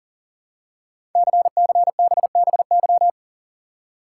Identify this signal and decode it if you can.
Morse code, heard as a keyed tone, about 42 wpm, 690 Hz. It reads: CCBBY
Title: Code Groups - 5 Letters - 40wpm